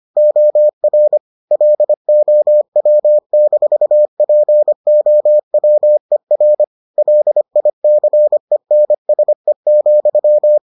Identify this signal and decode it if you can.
Morse code, heard as a keyed tone, about 25 wpm, 600 Hz. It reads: OR LOW-POWER LICENSE,